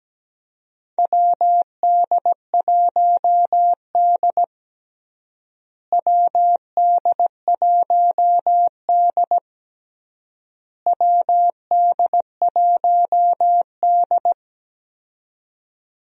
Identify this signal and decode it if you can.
Morse code, heard as a keyed tone, about 17 wpm, 700 Hz. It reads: WD1D WD1D WD1D
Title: USA Call Signs - 4 Characters - Repeated 3x - 17wpm